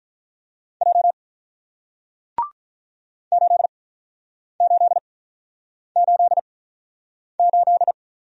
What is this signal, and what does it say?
Morse code, heard as a keyed tone, about 46 wpm, 700 Hz. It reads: J 8 8 8 8